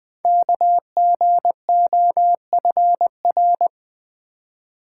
Morse code, heard as a keyed tone, about 20 wpm, 700 Hz. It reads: KGOFR
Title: Code Groups - 5 Letters - 20wpm